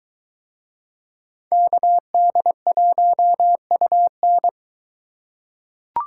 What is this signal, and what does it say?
Morse code, heard as a keyed tone, about 20 wpm, 700 Hz. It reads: KD1UN